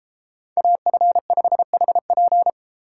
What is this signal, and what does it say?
Morse code, heard as a keyed tone, about 32 wpm, 700 Hz. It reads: AF5HP